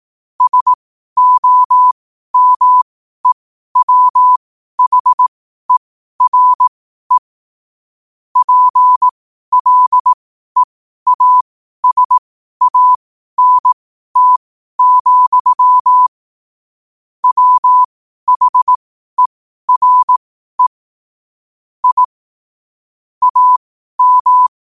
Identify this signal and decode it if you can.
Morse code, heard as a keyed tone, about 18 wpm, spaced as for 12 wpm, 1000 Hz. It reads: SOMEWHERE PLEASANT, WHERE I AM
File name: qotdfiles / 12wpm